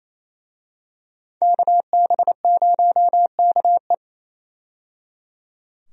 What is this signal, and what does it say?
Morse code, heard as a keyed tone, about 28 wpm, 700 Hz. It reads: KB0KE
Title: USA Call Signs - 5 Characters - 28wpm